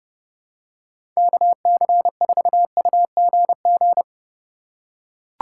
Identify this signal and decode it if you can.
Morse code, heard as a keyed tone, about 32 wpm, 700 Hz. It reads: KC4UGG
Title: USA Call Signs - 6 Characters - 3x Character Spacing - 30wpm